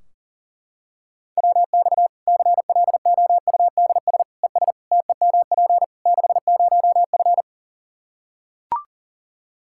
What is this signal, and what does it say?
Morse code, heard as a keyed tone, about 40 wpm, 700 Hz. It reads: WX CLOUDS ES TEMP 60F
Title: Practice QSO - Exchange 2 - Info - 40wpm